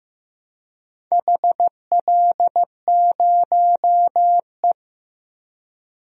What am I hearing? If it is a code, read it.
Morse code, heard as a keyed tone, about 16 wpm, 700 Hz. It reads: HL0E
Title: DXCC-100 Call Signs - 4 Characters - Repeated 2x - 15wpm